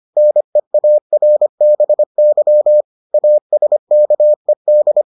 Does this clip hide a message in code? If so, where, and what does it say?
Yes: Morse code from 0.2 to 5.0 s, heard as a keyed tone, about 25 wpm, 600 Hz. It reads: NEARBY ASKED